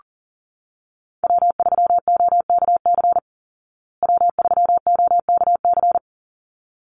Morse code, heard as a keyed tone, about 40 wpm, 700 Hz. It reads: W3OKC W3OKC
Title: Call Signs Repeated 3x - 40wpm